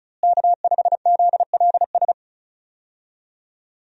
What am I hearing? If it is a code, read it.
Morse code, heard as a keyed tone, about 36 wpm, 700 Hz. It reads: K5ZLS
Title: USA Call Signs - 5 Characters - Repeated 3x - 35wpm